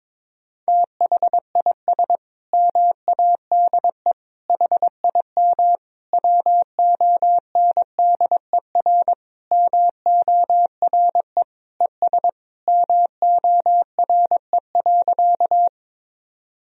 Morse code, heard as a keyed tone, about 22 wpm, 700 Hz. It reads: THIS MADE HIM WONDER MORE ES MORE.
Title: Sentences from Top 500 Words and Abbreviations 22wpm